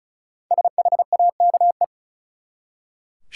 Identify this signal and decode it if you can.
Morse code, heard as a keyed tone, about 34 wpm, 700 Hz. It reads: SHAKE
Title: Five Letter Words - 35wpm